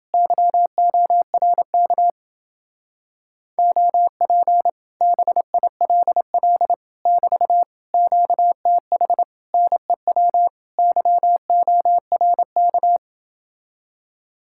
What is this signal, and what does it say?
Morse code, heard as a keyed tone, about 29 wpm, 700 Hz. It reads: YORK OP BILL <BT> QTH NEW YORK